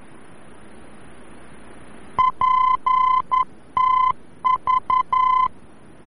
Morse code, heard as a keyed tone, about 11 wpm, 1020 Hz. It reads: PTV
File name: france / mx / PTV-VOR-F